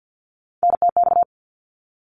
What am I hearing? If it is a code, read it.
Morse code, heard as a keyed tone, about 50 wpm, 700 Hz. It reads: NTX
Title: ARRL Sections 50wpm